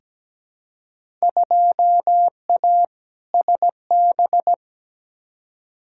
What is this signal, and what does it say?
Morse code, heard as a keyed tone, about 17 wpm, 700 Hz. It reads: 2A SB
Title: Field Day Exchanges - 17wpm